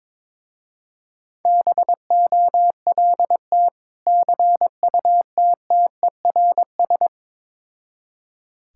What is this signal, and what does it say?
Morse code, heard as a keyed tone, about 22 wpm, 700 Hz. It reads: BOLT CUTTERS
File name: Difficult Words with Hints - 22wpm